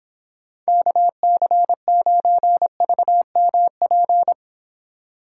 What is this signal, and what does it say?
Morse code, heard as a keyed tone, about 32 wpm, 700 Hz. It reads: KC9VMP